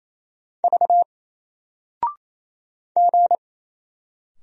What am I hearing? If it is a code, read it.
Morse code, heard as a keyed tone, about 28 wpm, 700 Hz. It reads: V G
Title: Single Letter-Number Alt 28wpm